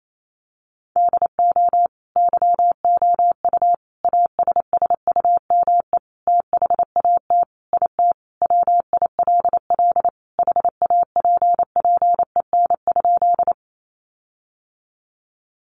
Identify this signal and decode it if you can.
Morse code, heard as a keyed tone, about 28 wpm, 700 Hz. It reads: DO YOU ASSUME THAT IT WILL HAPPEN?
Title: Sentences from Top 1500 Words 28wpm